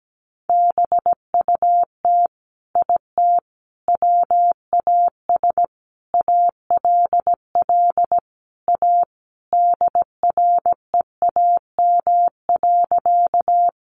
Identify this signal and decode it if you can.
Morse code, heard as a keyed tone, about 17 wpm, 700 Hz. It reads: BUT IT WAS ALL A DREAM.
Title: Sentences from Top 1500 Words 17wpm